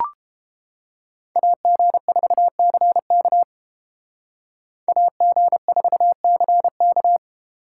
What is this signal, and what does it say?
Morse code, heard as a keyed tone, about 30 wpm, 700 Hz. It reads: AG4CK AG4CK